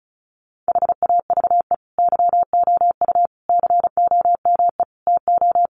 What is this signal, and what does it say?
Morse code, heard as a keyed tone, about 34 wpm, 700 Hz. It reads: HAVE YOU COME TO
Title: Sentences from Top 1400 Words 35wpm